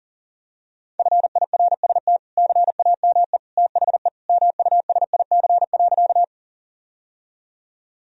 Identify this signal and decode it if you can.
Morse code, heard as a keyed tone, about 40 wpm, 700 Hz. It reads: FIRST CAME THE MUSIC.
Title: Sentences from Top 1000 Words - Review Part 1 - 40wpm